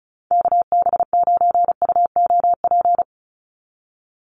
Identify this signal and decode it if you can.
Morse code, heard as a keyed tone, about 34 wpm, 700 Hz. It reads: KB9UOP
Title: USA Call Signs 35wpm